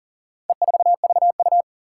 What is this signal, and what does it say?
Morse code, heard as a keyed tone, about 40 wpm, 700 Hz. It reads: E4VU
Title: Four Challenging Letter-Number Combinations 40wpm